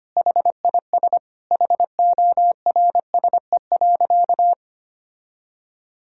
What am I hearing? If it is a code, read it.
Morse code, heard as a keyed tone, about 25 wpm, 700 Hz. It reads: HIS HORSE.
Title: Sentences from Top 1000 Words - Review Part 3 - 25wpm